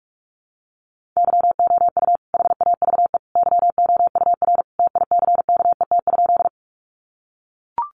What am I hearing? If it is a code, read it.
Morse code, heard as a keyed tone, about 44 wpm, 700 Hz. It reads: YOU HAVE YOUR TICKET?
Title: Sentences from Top 1400 Words 45wpm